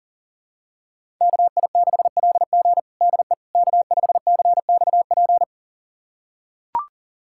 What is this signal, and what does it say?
Morse code, heard as a keyed tone, about 40 wpm, 700 Hz. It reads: KI6LG DE K5CXP
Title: USA Callsign DE Callsign - 5 Characters - 40wpm